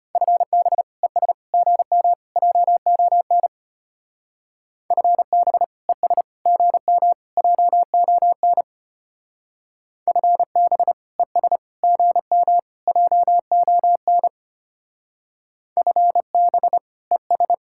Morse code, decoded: FB ES GM JON FB ES GM JON FB ES GM JON FB ES

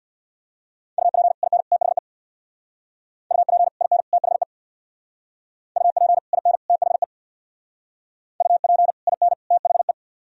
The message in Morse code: UP IN THE UP IN THE UP IN THE UP IN THE

